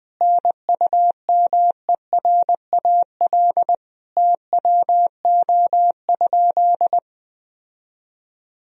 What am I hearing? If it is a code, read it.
Morse code, heard as a keyed tone, about 20 wpm, 700 Hz. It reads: NUMERAL TWO?
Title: Sentences from Top 1000 Words - Review Part 2 - 20wpm